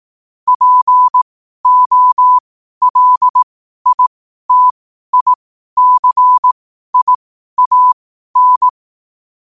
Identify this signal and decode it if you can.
Morse code, heard as a keyed tone, about 18 wpm, spaced as for 12 wpm, 1000 Hz. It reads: POLITICIAN
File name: qotdfiles / 12wpm